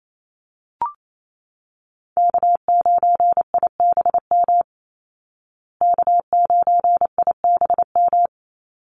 Morse code, heard as a keyed tone, about 28 wpm, 700 Hz. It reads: K9IBM K9IBM
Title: Call Signs Repeated 2x - 28wpm